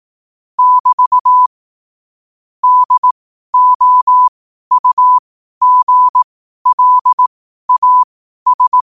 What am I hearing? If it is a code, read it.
Morse code, heard as a keyed tone, about 18 wpm, spaced as for 12 wpm, 1000 Hz. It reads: <BT> DOUGLAS
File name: qotdfiles / 12wpm